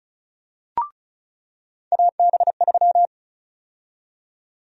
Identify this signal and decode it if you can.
Morse code, heard as a keyed tone, about 36 wpm, 700 Hz. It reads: AB3
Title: USA Call Sign Prefixes - 35wpm